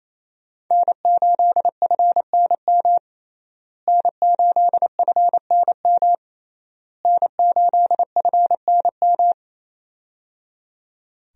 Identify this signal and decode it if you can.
Morse code, heard as a keyed tone, about 28 wpm, 700 Hz. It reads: N8FNM N8FNM N8FNM
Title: USA Call Signs - 5 Characters - Repeated 3x - 28wpm